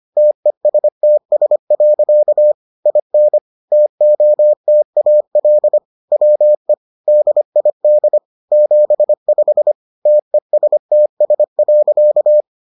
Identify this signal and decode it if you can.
Morse code, heard as a keyed tone, about 25 wpm, 600 Hz. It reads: TESTS. IN TOTAL WE DID 75 TESTS.